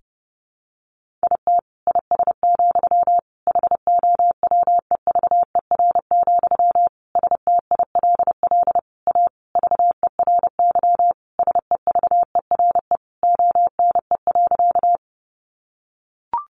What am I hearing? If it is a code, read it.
Morse code, heard as a keyed tone, about 30 wpm, 700 Hz. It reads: IT IS, HOWEVER, STILL A VERY SEVERE ONE.
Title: Sentences from Top 1400 Words 30wpm